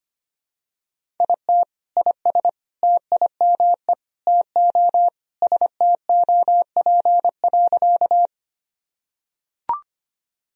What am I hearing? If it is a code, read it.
Morse code, heard as a keyed tone, about 25 wpm, 700 Hz. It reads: IT IS TIME TO STOP.
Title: Sentences from Top 1000 Words - Review Part 1 - 25wpm